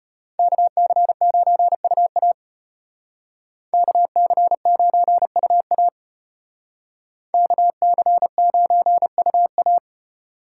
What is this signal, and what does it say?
Morse code, heard as a keyed tone, about 32 wpm, 700 Hz. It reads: KC9UA KC9UA KC9UA